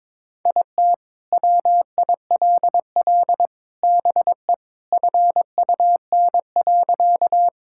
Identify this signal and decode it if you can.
Morse code, heard as a keyed tone, about 21 wpm, 700 Hz. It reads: IT WILL BE FUN.
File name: practice / 22wpm / Sentences with 3-Word Phrases - 22wpm